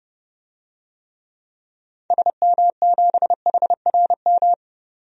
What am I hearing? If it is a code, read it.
Morse code, heard as a keyed tone, about 30 wpm, 700 Hz. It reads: SM7HRM
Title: DXCC-100 Call Signs - 6 Characters - 30wpm